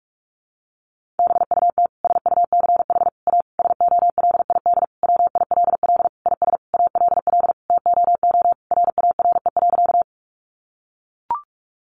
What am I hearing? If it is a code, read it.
Morse code, heard as a keyed tone, about 44 wpm, 700 Hz. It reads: BUT SUCH A SOLID WILL IS ALL TOO RARE.
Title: Sentences from Top 1500 Words 45wpm